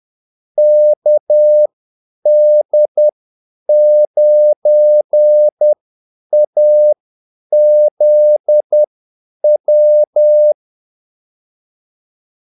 Morse code, heard as a keyed tone, about 10 wpm, spaced as for 8 wpm, 600 Hz. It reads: KD9AZW
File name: Callsigns-10-8wpm